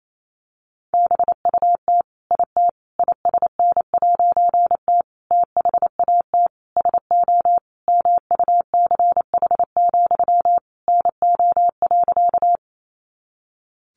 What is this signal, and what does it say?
Morse code, heard as a keyed tone, about 28 wpm, 700 Hz. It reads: BUT IT ISN'T THAT SO MUCH, NO.
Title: A Sound Of Thunder 28wpm